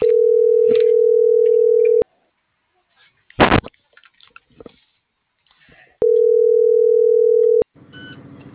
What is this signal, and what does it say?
Ringback tone (North America), 440 + 480 Hz.